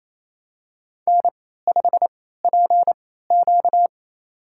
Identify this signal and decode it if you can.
Morse code, heard as a keyed tone, about 28 wpm, spaced as for 16 wpm, 700 Hz. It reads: N5PQ
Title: USA Call Signs - 3x Character Spacing - 28wpm